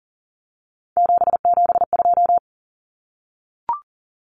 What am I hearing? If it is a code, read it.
Morse code, heard as a keyed tone, about 40 wpm, 700 Hz. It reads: 772
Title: Numbers - 3-Digit 40wpm